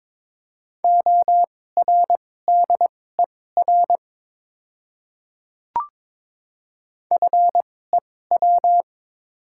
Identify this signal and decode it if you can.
Morse code, heard as a keyed tone, about 22 wpm, spaced as for 16 wpm, 700 Hz. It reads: ORDER FEW
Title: 2x Character Spacing - Top 500 Words - 22wpm